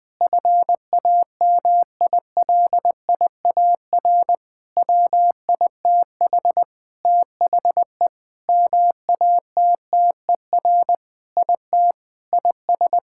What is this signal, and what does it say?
Morse code, heard as a keyed tone, about 20 wpm, 700 Hz. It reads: FAMILIAR WITH THE MATTER IT IS
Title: Sentences with 7-Word Phrases - 20wpm